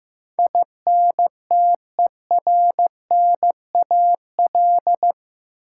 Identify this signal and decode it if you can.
Morse code, heard as a keyed tone, about 15 wpm, 700 Hz. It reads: INTERNAL
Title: Sentences from Top 1400 Words - New Word Drill - 15wpm